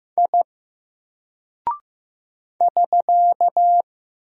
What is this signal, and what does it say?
Morse code, heard as a keyed tone, about 15 wpm, 700 Hz. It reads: I <SK>
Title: Single Character (Letter:Number:Punctuation:Prosign) 15wpm